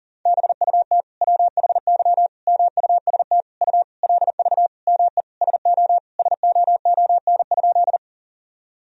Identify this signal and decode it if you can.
Morse code, heard as a keyed tone, about 40 wpm, 700 Hz. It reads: BUT WHY MUST U LV ME SO SOON?
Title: Sentences from Top 400 Words and Abbreviations 40wpm